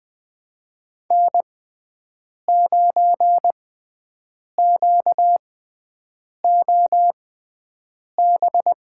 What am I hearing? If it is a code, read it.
Morse code, heard as a keyed tone, about 20 wpm, spaced as for 7 wpm, 700 Hz. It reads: N9QOB
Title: USA Call Signs - 5 Characters - 6x Character Spacing - 20wpm